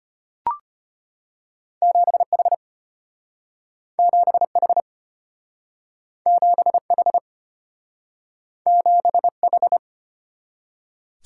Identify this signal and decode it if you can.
Morse code, heard as a keyed tone, about 34 wpm, 700 Hz. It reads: 7H 7H 7H 7H